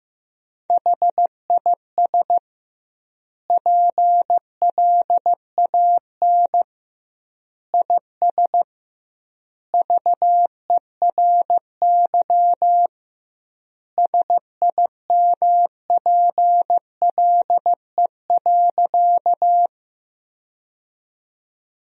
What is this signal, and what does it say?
Morse code, heard as a keyed tone, about 15 wpm, 700 Hz. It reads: HIS PLAN IS VERY SIMPLE.
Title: 2x Word-Spacing - Sentences from Top 500 Words - 15wpm